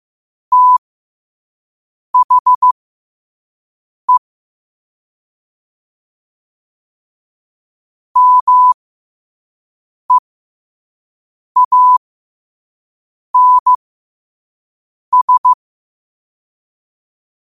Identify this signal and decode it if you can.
Morse code, heard as a keyed tone, about 15 wpm, spaced as for 5 wpm, 1000 Hz. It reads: THE MEANS
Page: New quotes every day in morse code at 5 Words per minute.